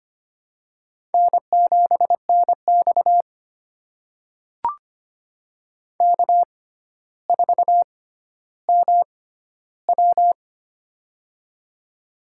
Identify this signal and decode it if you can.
Morse code, heard as a keyed tone, about 25 wpm, spaced as for 13 wpm, 700 Hz. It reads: N7NX K4MW
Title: USA Call Signs - 4 Characters - 6x Character Spacing - 25wpm